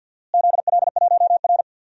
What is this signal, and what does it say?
Morse code, heard as a keyed tone, about 52 wpm, 700 Hz. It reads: ZL1R
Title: DXCC-100 Call Signs - 4 Characters - Repeated 3x - 50wpm